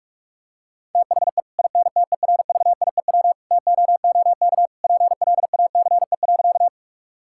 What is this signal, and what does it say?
Morse code, heard as a keyed tone, about 41 wpm, 700 Hz. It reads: THE INTERVIEW TOOK PLACE.
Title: Sentences from Top 1600 Words 45wpm